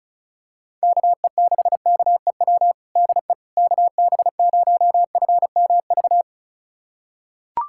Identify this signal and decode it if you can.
Morse code, heard as a keyed tone, about 36 wpm, 700 Hz. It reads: KE6KEW DE KB0FMV
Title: USA Callsign DE Callsign 35wpm